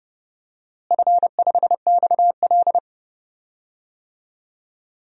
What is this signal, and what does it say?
Morse code, heard as a keyed tone, about 30 wpm, 700 Hz. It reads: F5XL
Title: DXCC-100 Call Signs - 4 Characters - Repeated 3x - 30wpm